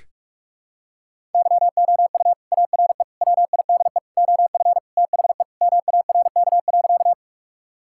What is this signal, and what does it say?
Morse code, heard as a keyed tone, about 41 wpm, 700 Hz. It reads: YOU ARE WIDE OF THE MARK.
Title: Sentences from Top 1000 Words - Review Part 6 - 45wpm